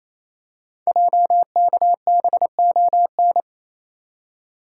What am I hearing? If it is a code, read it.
Morse code, heard as a keyed tone, about 29 wpm, 700 Hz. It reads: JKBON
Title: Code Groups - 5 Letter:Numbers - 28wpm